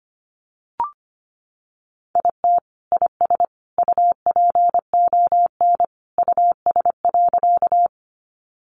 Morse code, heard as a keyed tone, about 25 wpm, 700 Hz. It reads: IT IS UPON US.
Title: Sentences from Top 1100 Words 25wpm